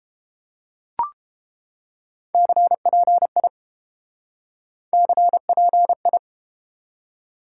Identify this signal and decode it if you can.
Morse code, heard as a keyed tone, about 30 wpm, 700 Hz. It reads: CPI CPI